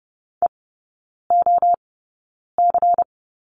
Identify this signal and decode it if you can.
Morse code, heard as a keyed tone, about 30 wpm, spaced as for 9 wpm, 700 Hz. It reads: EOC
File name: Lesson 18e - Letter m - Three Characters 30wpm